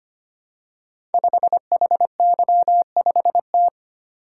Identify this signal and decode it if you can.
Morse code, heard as a keyed tone, about 25 wpm, 700 Hz. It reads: 5HY5T
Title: Code Groups - 5 Letter:Numbers - 25wpm